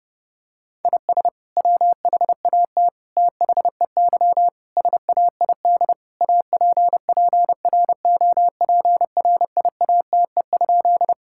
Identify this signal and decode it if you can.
Morse code, heard as a keyed tone, about 28 wpm, 700 Hz. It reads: IS WHAT THEY SAID APPROPRIATE?
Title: Sentences from Top 1000 Words - Review Part 5 - 30wpm